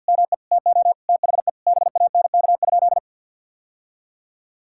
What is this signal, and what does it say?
Morse code, heard as a keyed tone, about 50 wpm, 700 Hz. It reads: ME TO THE BANK?
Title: Sentences from Top 1000 Words - Review Part 6 - 50wpm